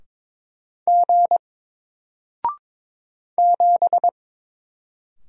Morse code, heard as a keyed tone, about 22 wpm, 700 Hz. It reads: G 7
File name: Single-Letter-Number-22wpm